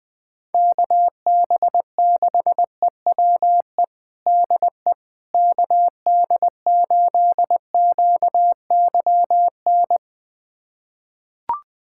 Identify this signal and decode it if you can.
Morse code, heard as a keyed tone, about 21 wpm, 700 Hz. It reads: KB6EWE DE KD8QYN
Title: USA Callsign DE Callsign - 6 Characters - 20wpm